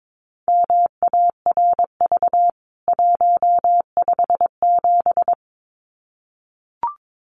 Morse code, heard as a keyed tone, about 22 wpm, 700 Hz. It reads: MARV 157
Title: CWT Exchange 22wpm